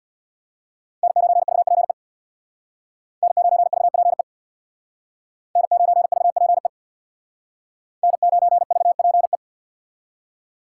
Morse code, heard as a keyed tone, about 65 wpm, 700 Hz. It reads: N9VPE N9VPE N9VPE N9VPE